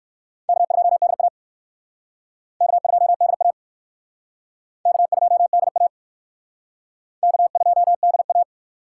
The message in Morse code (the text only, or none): K2DA K2DA K2DA K2DA